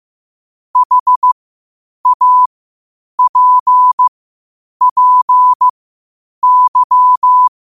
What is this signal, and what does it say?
Morse code, heard as a keyed tone, about 15 wpm, spaced as for 8 wpm, 1000 Hz. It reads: HAPPY
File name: qotdfiles / 8wpm